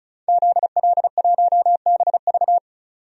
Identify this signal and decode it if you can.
Morse code, heard as a keyed tone, about 36 wpm, 700 Hz. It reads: ZL1BV
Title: DXCC-100 Call Signs - 5 Characters - 35wpm